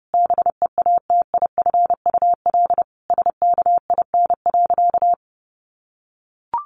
Morse code, heard as a keyed tone, about 30 wpm, 700 Hz. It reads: BEATIFUL SKIN.
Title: Sentences from Top 1300 Words 30wpm